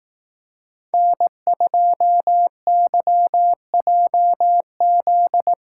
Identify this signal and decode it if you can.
Morse code, heard as a keyed tone, about 18 wpm, 700 Hz. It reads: N2YJZ
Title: Speed Racing - USA 1x3 Call Signs 15wpm